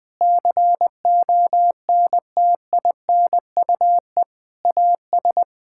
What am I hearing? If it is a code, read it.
Morse code, heard as a keyed tone, about 20 wpm, 700 Hz. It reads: CONTINUE AS
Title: Sentences from Top 1000 Words - Review Part 6 - 20wpm